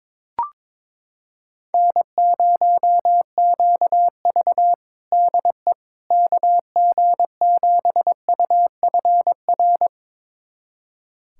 Morse code, heard as a keyed tone, about 22 wpm, 700 Hz. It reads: N0QV DE KG7UFR
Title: USA Callsign DE Callsign 22wpm